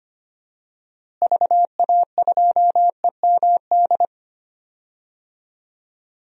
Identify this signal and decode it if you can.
Morse code, heard as a keyed tone, about 25 wpm, 700 Hz. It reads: VA2EMD
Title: DXCC-100 Call Signs - 6 Characters - Repeated 3x - 25wpm